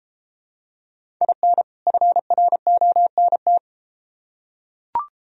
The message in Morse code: IN FRONT

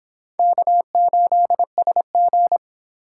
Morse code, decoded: K8SG

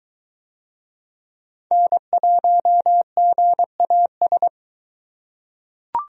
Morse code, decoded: N1GAS

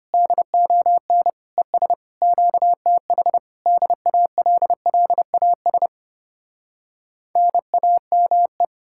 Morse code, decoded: DON ES QTH DALLAS NAME

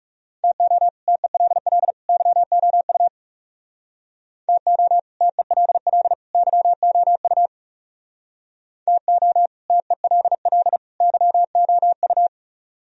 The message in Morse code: TO TELL YOU TO TELL YOU TO TELL YOU

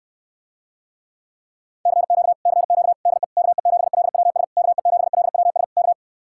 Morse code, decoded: CQ CQ DE KE7LWS KE7LWS K

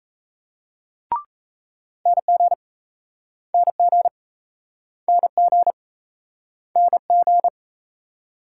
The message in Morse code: NG NG NG NG

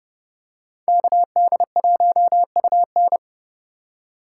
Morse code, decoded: KD1UN

